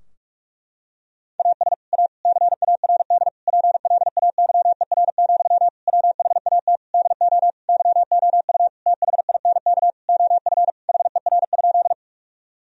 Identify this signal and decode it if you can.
Morse code, heard as a keyed tone, about 46 wpm, 700 Hz. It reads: AS A CARD PLAYER, WHAT DO YOU THINK OF HER?
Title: Sentences from Top 1000 Words - Review Part 4 - 45wpm